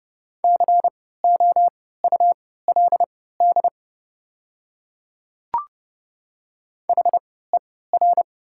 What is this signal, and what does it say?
Morse code, heard as a keyed tone, about 30 wpm, spaced as for 17 wpm, 700 Hz. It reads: COULD HER
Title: 3x Character Spacing - Top 100 Words - 30wpm